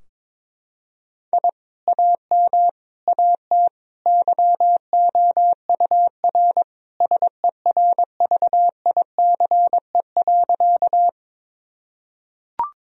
Morse code, heard as a keyed tone, about 21 wpm, 700 Hz. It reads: I AM AT YOUR SERVICE.
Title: Sentences from Top 1000 Words - Review Part 4 - 22wpm